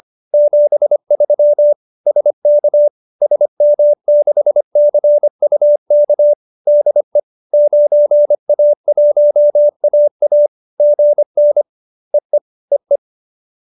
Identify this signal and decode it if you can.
Morse code, heard as a keyed tone, about 25 wpm, 590 Hz. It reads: 73 SK SM6CUK DE 9A1AA GN EE EE